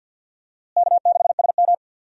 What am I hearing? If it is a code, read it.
Morse code, heard as a keyed tone, about 50 wpm, 700 Hz. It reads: K6SM
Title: USA Call Signs - 4 Characters - Repeated 2x - 50wpm